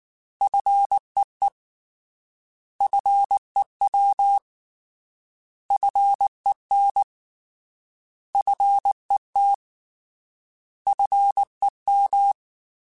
Morse code, heard as a keyed tone, about 19 wpm, 800 Hz. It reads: FEE FEW FEN FET FEM